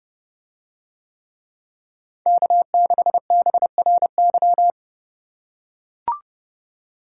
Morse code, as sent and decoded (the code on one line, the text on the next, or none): -.- -.... -... .-. -.--
K6BRY